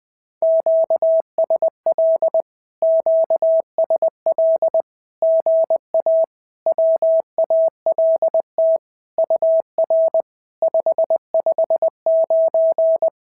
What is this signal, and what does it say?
Morse code, heard as a keyed tone, about 20 wpm, 650 Hz. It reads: QSL QSL GA WALT UR 559
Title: SOTA QSO Practice 20wpm